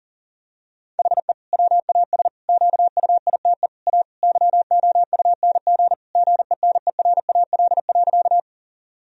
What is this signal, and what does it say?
Morse code, heard as a keyed tone, about 40 wpm, 700 Hz. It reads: HE WAS QUITE A YOUNG GENERAL.
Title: Sentences from Top 1000 Words - Review Part 3 - 40wpm